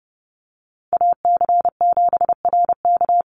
Morse code, heard as a keyed tone, about 30 wpm, 700 Hz. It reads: AC7RK
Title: USA Call Signs 30wpm